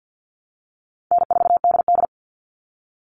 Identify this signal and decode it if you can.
Morse code, heard as a keyed tone, about 50 wpm, 700 Hz. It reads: N4DD
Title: All Cut Numbers - 4-Digit 50wpm